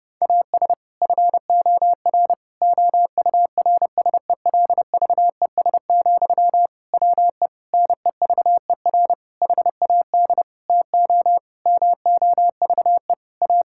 Morse code, heard as a keyed tone, about 30 wpm, 700 Hz. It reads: AS FOR OURSELVES, WE NEVER HAD TO MOVE A
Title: Sentences from Top 1600 Words 30wpm